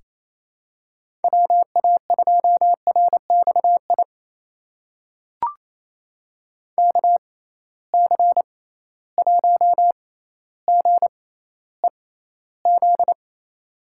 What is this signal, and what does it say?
Morse code, heard as a keyed tone, about 28 wpm, spaced as for 14 wpm, 700 Hz. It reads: WA2RXI KC1GEZ
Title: USA Call Signs - 6 Characters - 6x Character Spacing - 28wpm